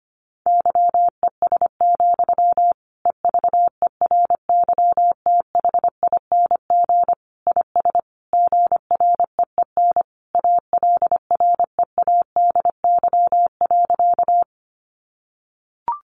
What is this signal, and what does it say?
Morse code, heard as a keyed tone, about 25 wpm, 700 Hz. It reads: YES, EVERYTHING IS GREEN ALREADY.
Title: Sentences from Top 700 Words 25wpm